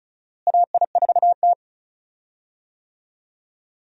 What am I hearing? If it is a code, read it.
Morse code, heard as a keyed tone, about 36 wpm, 700 Hz. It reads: AI4T
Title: USA Call Signs - 4 Characters - Repeated 2x - 35wpm